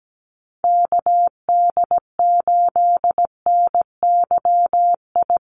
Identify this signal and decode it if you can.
Morse code, heard as a keyed tone, about 17 wpm, 700 Hz. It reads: KD8NYI
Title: USA Call Signs Repeated 3x 17wpm